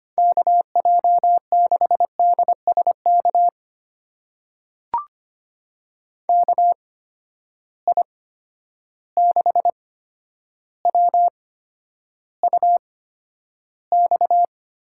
Morse code, heard as a keyed tone, about 25 wpm, spaced as for 11 wpm, 700 Hz. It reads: KJ6DSK KI6WUX
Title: USA Call Signs - 6 Characters - 8x Character Spacing - 25wpm